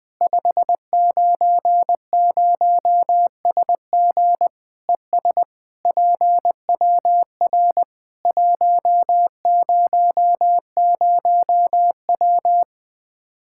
Morse code, heard as a keyed tone, about 21 wpm, 700 Hz. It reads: 590SG ES PWR 100W
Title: K1USN SST QSO Practice 20wpm